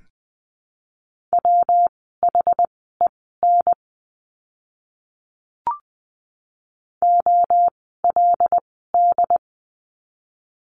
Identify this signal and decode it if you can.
Morse code, heard as a keyed tone, about 20 wpm, spaced as for 14 wpm, 700 Hz. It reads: WHEN OLD
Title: 230 Lesson 37f - Letter x - Words 20wpm